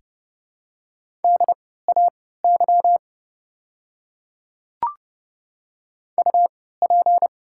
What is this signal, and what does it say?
Morse code, heard as a keyed tone, about 32 wpm, spaced as for 17 wpm, 700 Hz. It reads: DAY UP
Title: 3x Character Spacing - Top 100 Words - 30wpm